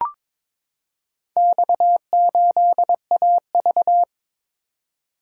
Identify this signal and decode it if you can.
Morse code, heard as a keyed tone, about 22 wpm, 700 Hz. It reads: X8AV
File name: practice / 22wpm / Four Letter-Number Combinations - 22wpm